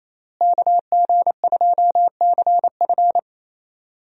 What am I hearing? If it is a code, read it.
Morse code, heard as a keyed tone, about 28 wpm, 700 Hz. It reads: KG2CF
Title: USA Call Signs - 5 Characters - Repeated 3x - 28wpm